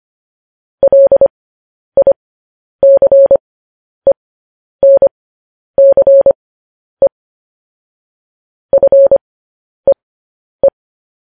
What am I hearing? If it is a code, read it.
Morse code, heard as a keyed tone, about 25 wpm, spaced as for 10 wpm, 550 Hz. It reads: LICENCE FEE
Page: news headlines in morse code at 10 wpm